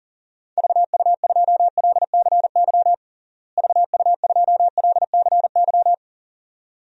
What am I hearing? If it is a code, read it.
Morse code, heard as a keyed tone, about 40 wpm, 700 Hz. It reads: VU2LCY VU2LCY
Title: DXCC-100 Call Signs - 6 Characters - Repeated 2x - 40wpm